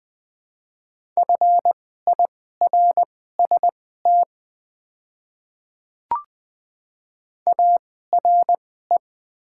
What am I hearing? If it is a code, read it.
Morse code, heard as a keyed tone, about 20 wpm, spaced as for 15 wpm, 700 Hz. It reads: FIRST ARE
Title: 2x Character Spacing - Top 200 Words - 20wpm